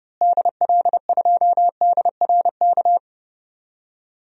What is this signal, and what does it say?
Morse code, heard as a keyed tone, about 30 wpm, 700 Hz. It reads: DL2DRK
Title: DXCC-100 Call Signs - 6 Characters - Repeated 2x - 30wpm